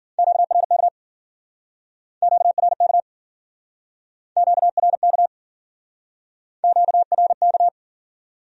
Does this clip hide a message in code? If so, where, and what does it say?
Yes: Morse code from 0.2 to 7.7 s, heard as a keyed tone, about 52 wpm, 700 Hz. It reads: QRK QRK QRK QRK